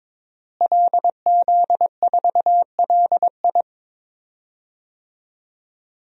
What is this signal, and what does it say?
Morse code, heard as a keyed tone, about 22 wpm, 700 Hz. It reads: LZ4LI
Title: DXCC-100 Call Signs - 5 Characters - Repeated 3x - 22wpm